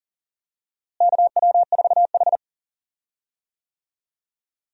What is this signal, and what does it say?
Morse code, heard as a keyed tone, about 40 wpm, 700 Hz. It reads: KW4H
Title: USA Call Signs - 4 Characters - Repeated 3x - 40wpm